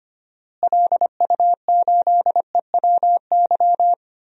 Morse code, heard as a keyed tone, about 25 wpm, 700 Hz. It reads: LU8EWY
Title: DXCC-100 Call Signs - 6 Characters - Repeated 2x - 25wpm